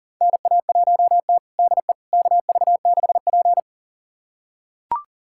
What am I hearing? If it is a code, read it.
Morse code, heard as a keyed tone, about 40 wpm, 690 Hz. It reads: NA1T DE KV6P
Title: USA Callsign DE Callsign - 4 Characters - 40wpm